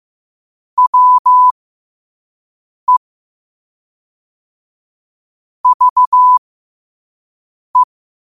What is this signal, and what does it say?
Morse code, heard as a keyed tone, about 15 wpm, spaced as for 5 wpm, 1000 Hz. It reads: WE VE
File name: qotdfiles / 5wpm